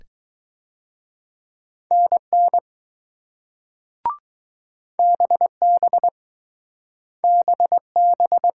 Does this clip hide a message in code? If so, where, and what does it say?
Yes: Morse code from 1.9 to 8.5 s, heard as a keyed tone, about 20 wpm, 700 Hz. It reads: NN BB BB